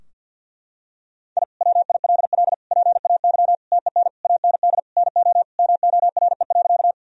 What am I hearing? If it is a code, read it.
Morse code, heard as a keyed tone, about 51 wpm, 700 Hz. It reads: I WILL PAY TEN AND NO MORE.
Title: Sentences from Top 1000 Words - Review Part 2 - 50wpm